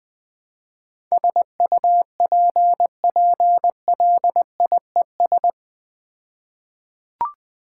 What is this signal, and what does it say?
Morse code, heard as a keyed tone, about 20 wpm, 700 Hz. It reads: SUPPLIES
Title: All-Age Appropriate Jokes - 20wpm